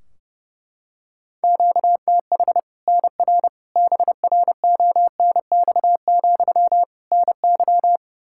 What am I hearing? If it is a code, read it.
Morse code, heard as a keyed tone, about 29 wpm, 700 Hz. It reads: QTH NR BRONX, NY
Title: Practice QSO - Exchange 1 - Beginning - 30wpm